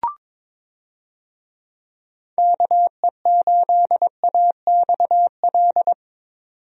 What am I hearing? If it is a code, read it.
Morse code, heard as a keyed tone, about 22 wpm, 700 Hz. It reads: KE8AXL
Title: USA Call Signs - 6 Characters - 22wpm